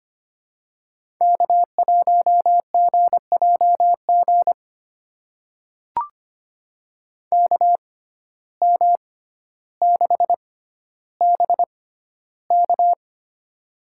Morse code, heard as a keyed tone, about 25 wpm, spaced as for 13 wpm, 700 Hz. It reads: K1GJG KM6BK
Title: USA Call Signs - 5 Characters - 6x Character Spacing - 25wpm